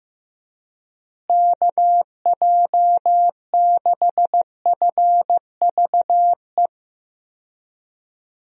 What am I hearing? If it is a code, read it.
Morse code, heard as a keyed tone, about 15 wpm, 700 Hz. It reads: KJ6FVE
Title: USA Call Signs - 6 Characters - Repeated 3x - 15wpm